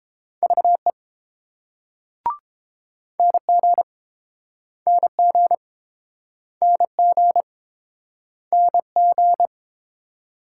Morse code, heard as a keyed tone, about 29 wpm, 700 Hz. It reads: VE NG NG NG NG